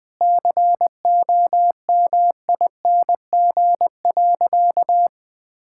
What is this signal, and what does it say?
Morse code, heard as a keyed tone, about 21 wpm, 690 Hz. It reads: COMING.
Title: Sentences with 4-Word Phrases - Spoken Beginning - 20wpm